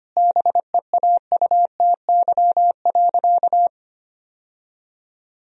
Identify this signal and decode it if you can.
Morse code, heard as a keyed tone, about 25 wpm, 700 Hz. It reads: BEAUTY.
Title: Sentences from Top 1000 Words - Review Part 2 - 25wpm